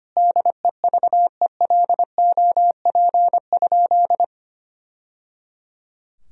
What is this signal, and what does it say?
Morse code, heard as a keyed tone, about 25 wpm, 700 Hz. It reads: DEVELOP?
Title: Sentences from Top 1000 Words - Review Part 2 - 25wpm